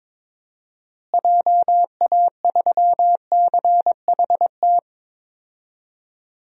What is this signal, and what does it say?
Morse code, heard as a keyed tone, about 23 wpm, 700 Hz. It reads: JA3CHT
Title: DXCC-100 Call Signs - 6 Characters - Repeated 3x - 22wpm